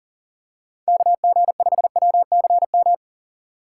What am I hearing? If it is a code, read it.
Morse code, heard as a keyed tone, about 37 wpm, 700 Hz. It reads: KG5WCM
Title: USA Call Signs - 6 Characters - 3x Character Spacing - 40wpm